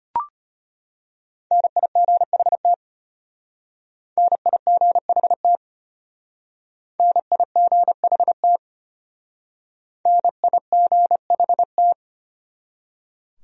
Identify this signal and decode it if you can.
Morse code, heard as a keyed tone, about 33 wpm, 700 Hz. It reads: NIGHT NIGHT NIGHT NIGHT